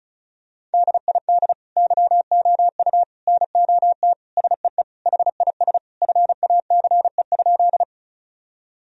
Morse code, heard as a keyed tone, about 36 wpm, 700 Hz. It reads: DID YOU NOT SEE HIS FACE?
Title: Sentences from Top 1000 Words - Review Part 2 - 35wpm